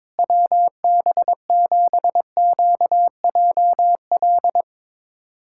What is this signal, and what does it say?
Morse code, heard as a keyed tone, about 22 wpm, 700 Hz. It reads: WB7QJL
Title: USA Callsign DE Callsign - 6 Characters - 22wpm